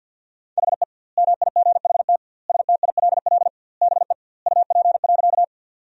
Morse code, heard as a keyed tone, about 50 wpm, 700 Hz. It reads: HE MIGHT STILL BE UP.
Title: 2x Word-Spacing - Sentences from Top 300 Words - 50wpm